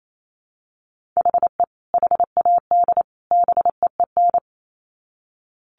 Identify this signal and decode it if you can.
Morse code, heard as a keyed tone, about 28 wpm, 700 Hz. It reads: HE HAD BEEN
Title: Sets of 3 Words 28wpm